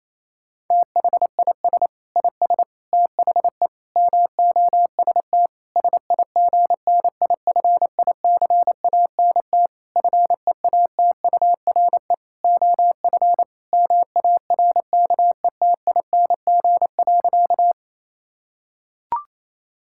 Morse code, heard as a keyed tone, about 28 wpm, 700 Hz. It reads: THIS IS THE MOST SIGNIFICANT FEATURE OF MARKETING.
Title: Sentences from Top 1000 Words - Review Part 6 - 28wpm